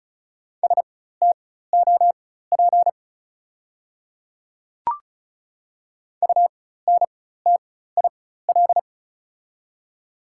Morse code, heard as a keyed tone, about 36 wpm, spaced as for 16 wpm, 690 Hz. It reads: STOP UNTIL
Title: 4x Character Spacing - Top 300 Words - 35wpm